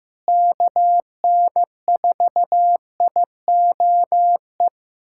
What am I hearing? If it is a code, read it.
Morse code, heard as a keyed tone, about 15 wpm, 700 Hz. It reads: KN4IOE
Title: USA Callsign DE Callsign 15wpm